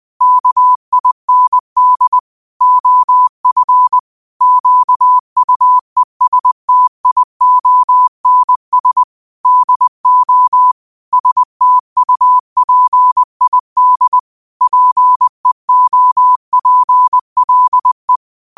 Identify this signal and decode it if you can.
Morse code, heard as a keyed tone, about 20 wpm, 1000 Hz. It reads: KIND OF QUESTIONS DO STUPID PEOPLE